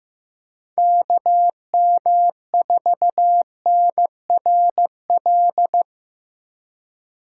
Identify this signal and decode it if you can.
Morse code, heard as a keyed tone, about 18 wpm, 700 Hz. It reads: KM4NRL
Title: Speed Racing - USA 2x3 Call Signs 15wpm